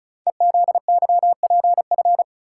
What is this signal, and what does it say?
Morse code, heard as a keyed tone, about 36 wpm, 690 Hz. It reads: EZYPF
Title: Code Groups - 5 Letters - 35wpm